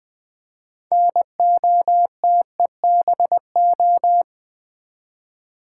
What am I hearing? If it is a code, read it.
Morse code, heard as a keyed tone, about 20 wpm, 700 Hz. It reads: NOTEBO
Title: Build A Word - 3 Letters At A Time - 8-Letter Words - 20wpm